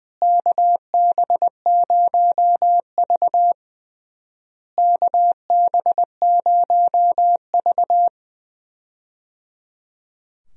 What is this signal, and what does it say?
Morse code, heard as a keyed tone, about 20 wpm, 700 Hz. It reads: KB0V KB0V
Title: USA Call Signs - 4 Characters - Repeated 3x - 20wpm